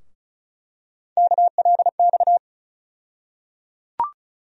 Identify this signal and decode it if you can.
Morse code, heard as a keyed tone, about 36 wpm, 700 Hz. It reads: KLX
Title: USA Call Sign Postfixes - 3 Letters 35wpm